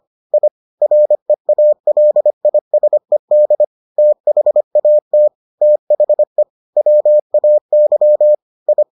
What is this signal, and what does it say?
Morse code, heard as a keyed tone, about 25 wpm, 600 Hz. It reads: I REALISED THAT THE WAY I